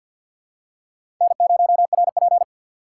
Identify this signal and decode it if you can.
Morse code, heard as a keyed tone, about 53 wpm, 700 Hz. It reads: N0RP
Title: USA Call Signs - 2x Character Spacing - 50wpm